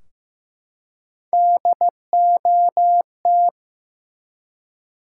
Morse code, heard as a keyed tone, about 15 wpm, 700 Hz. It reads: DOT
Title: Three Letter Words - 15wpm